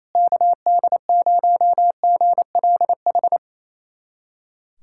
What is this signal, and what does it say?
Morse code, heard as a keyed tone, about 28 wpm, 700 Hz. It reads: KD0GLH
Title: USA Callsign DE Callsign 28wpm